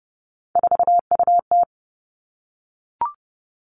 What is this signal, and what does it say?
Morse code, heard as a keyed tone, about 31 wpm, 700 Hz. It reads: 4UT
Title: Three Challenging Letter-Number Combinations 30wpm